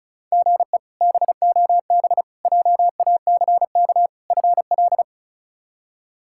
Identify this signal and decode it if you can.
Morse code, heard as a keyed tone, about 34 wpm, 700 Hz. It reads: GE BOB JACK FL
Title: K1USN SST QSO Practice - 35wpm